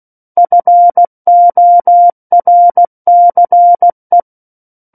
Morse code, heard as a keyed tone, about 16 wpm, 700 Hz. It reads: FORCE
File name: Book One / 07-Chapter 7 - 16 WPM